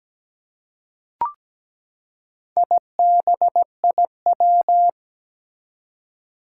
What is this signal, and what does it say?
Morse code, heard as a keyed tone, about 17 wpm, 700 Hz. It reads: IBIW
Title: Four Challenging Letter-Number Combinations 17wpm